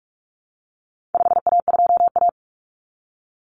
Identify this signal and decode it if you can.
Morse code, heard as a keyed tone, about 44 wpm, 700 Hz. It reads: 5A2A
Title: Common Cut Numbers - 4-Digit 45wpm